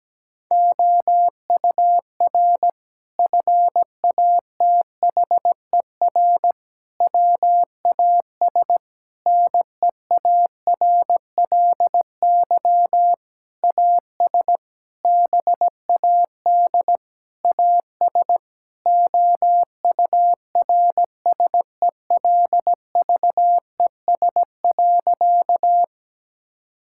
Morse code, heard as a keyed tone, about 17 wpm, 700 Hz. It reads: OUR FATHER WAS NEARLY AS BAD AS OURSELVES.
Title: Sentences from Top 1600 Words 17wpm